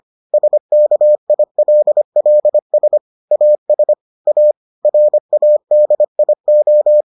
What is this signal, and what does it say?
Morse code, heard as a keyed tone, about 25 wpm, 600 Hz. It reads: SKILLS AS A RADIO